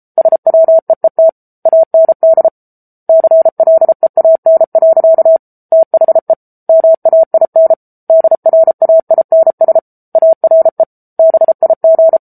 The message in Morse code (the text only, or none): SWEET AND CLEAN. THE MAIN DRAINS ARE BIG